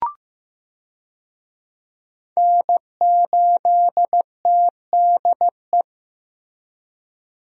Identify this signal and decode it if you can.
Morse code, heard as a keyed tone, about 15 wpm, 700 Hz. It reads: N8TDE
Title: USA Call Signs - 5 Characters - Repeated 3x - 15wpm